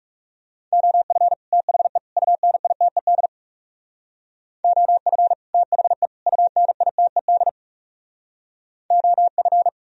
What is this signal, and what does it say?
Morse code, heard as a keyed tone, about 39 wpm, 700 Hz. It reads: OF THE UNITED OF THE UNITED OF